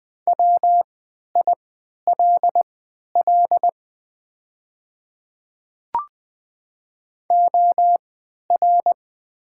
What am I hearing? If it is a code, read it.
Morse code, heard as a keyed tone, about 20 wpm, spaced as for 11 wpm, 700 Hz. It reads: WILL OR
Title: 3x Character Spacing - Top 100 Words - 20wpm